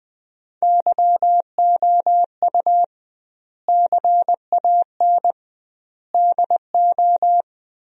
Morse code, heard as a keyed tone, about 20 wpm, 700 Hz. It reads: YOU CAN DO
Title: 2x Word-Spacing - Sentences from Top 100 Words - 20wpm